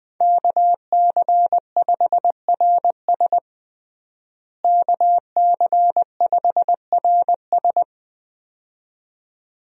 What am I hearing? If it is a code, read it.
Morse code, heard as a keyed tone, about 20 wpm, 700 Hz. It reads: KC5RS KC5RS
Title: USA Call Signs - 5 Characters - Repeated 2x - 20wpm